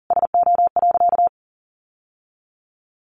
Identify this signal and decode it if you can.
Morse code, heard as a keyed tone, about 40 wpm, 700 Hz. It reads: SO.
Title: Sentences from Top 1000 Words 40wpm